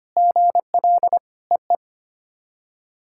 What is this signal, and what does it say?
Morse code, heard as a keyed tone, about 25 wpm, 700 Hz. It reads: GL EE
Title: SOTA QSO Practice 25wpm